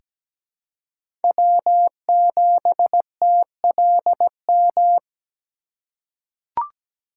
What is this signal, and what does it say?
Morse code, heard as a keyed tone, about 17 wpm, 700 Hz. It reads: W7TLM
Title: USA Call Signs - 5 Characters - 4x Character Spacing - 17wpm